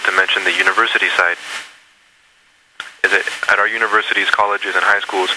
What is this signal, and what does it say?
Emergency Alert System (SAME) tones, heard as harsh buzzing bursts, with speech. It is heard from 0.9 to 5.1 s.